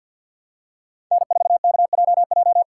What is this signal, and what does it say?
Morse code, heard as a keyed tone, about 46 wpm, 700 Hz. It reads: N4KJJ
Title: USA Callsign DE Callsign - 5 Characters - 50wpm